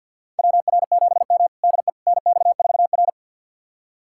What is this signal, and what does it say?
Morse code, heard as a keyed tone, about 51 wpm, 700 Hz. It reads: WR7M DE NX4R
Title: USA Callsign DE Callsign - 4 Characters - 50wpm